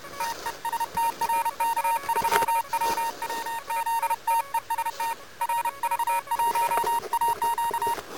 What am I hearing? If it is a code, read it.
Morse code, heard as a keyed tone, about 31 wpm, 910 Hz. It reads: TEST RK3WWZ TEST RV1AQ
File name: Expedition / 12.01.2013 RK3WWZ UA1CEG p test